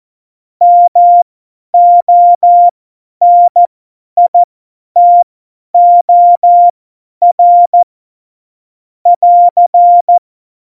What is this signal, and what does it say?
Morse code, heard as a keyed tone, about 14 wpm, spaced as for 10 wpm, 700 Hz. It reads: MONITOR <AR>
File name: Morsekurs / Witze / Witz 21. Januar 2020, 50 F70